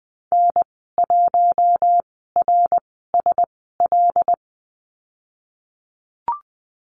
Morse code, heard as a keyed tone, about 20 wpm, spaced as for 15 wpm, 700 Hz. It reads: N1RSL
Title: 082 Lesson 14g - Number 2 - Call Signs 20wpm